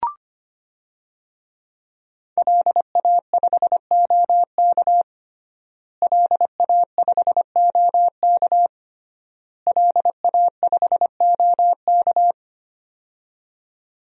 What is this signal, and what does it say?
Morse code, heard as a keyed tone, about 25 wpm, 700 Hz. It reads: LA5OK LA5OK LA5OK
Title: DXCC-100 Call Signs - 5 Characters - Repeated 3x - 25wpm